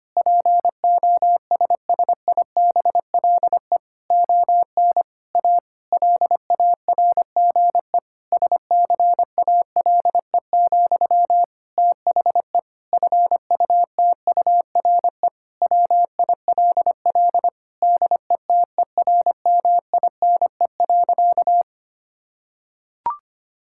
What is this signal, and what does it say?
Morse code, heard as a keyed tone, about 25 wpm, 700 Hz. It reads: POSSIBLE ON A LARGE SCALE, THE FUTURE WILL DETERMINE.
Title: Sentences from Top 1000 Words - Review Part 5 - 25wpm